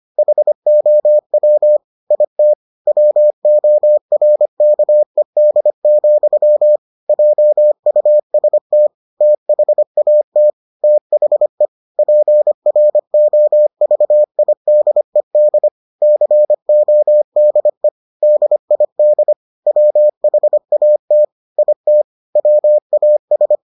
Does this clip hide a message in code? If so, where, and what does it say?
Yes: Morse code from 0.2 to 23.6 s, heard as a keyed tone, about 25 wpm, 600 Hz. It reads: HOW IT WORKED, JUST THAT THE PROVIDED CODE DID WHAT IT WAS